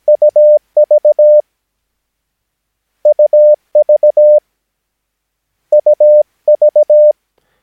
Morse code, heard as a keyed tone, about 17 wpm, 600 Hz. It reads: UV UV UV
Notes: Övningsmaterial med alfabetet A till Ö samt siffror 0 till 9 enligt International/Continental Morse Code.